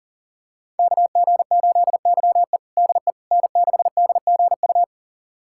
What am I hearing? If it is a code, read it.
Morse code, heard as a keyed tone, about 40 wpm, 700 Hz. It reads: KG8YE DE N6DGU
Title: USA Callsign DE Callsign - 5 Characters - 40wpm